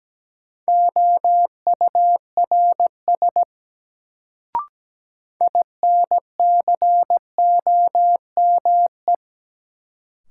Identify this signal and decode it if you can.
Morse code, heard as a keyed tone, about 17 wpm, 700 Hz. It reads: OURS INCOME
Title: Sentences from Top 1000 Words - Review Part 5 - New Word Drill - 17wpm